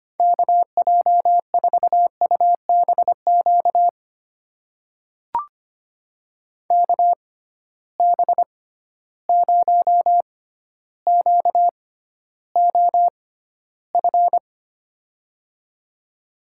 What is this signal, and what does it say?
Morse code, heard as a keyed tone, about 25 wpm, spaced as for 13 wpm, 700 Hz. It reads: KJ4UBQ KB0QOF
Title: USA Call Signs - 6 Characters - 6x Character Spacing - 25wpm